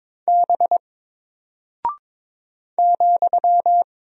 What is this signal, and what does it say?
Morse code, heard as a keyed tone, about 23 wpm, 700 Hz. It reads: B ,
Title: Single Character (Letter:Number:Punctuation:Prosign) 22wpm